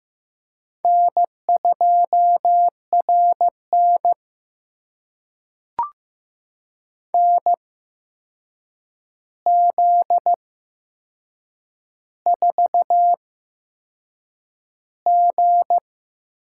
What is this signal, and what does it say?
Morse code, heard as a keyed tone, about 15 wpm, spaced as for 6 wpm, 700 Hz. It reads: N2RN NZ4G
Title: USA Call Signs - 4 Characters - 8x Character Spacing - 15wpm